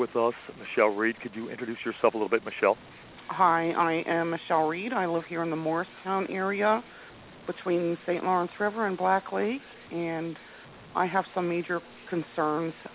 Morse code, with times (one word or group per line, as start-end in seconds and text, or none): none